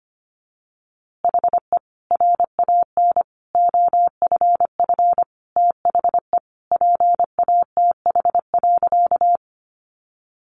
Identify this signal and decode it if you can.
Morse code, heard as a keyed tone, about 25 wpm, 700 Hz. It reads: HE RAN OFF THE PATH.
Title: A Sound Of Thunder 25wpm